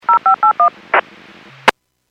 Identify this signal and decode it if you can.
DTMF, heard as telephone touch tones, tones of about 110 ms. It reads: 0581
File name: WPS 451.25 WPS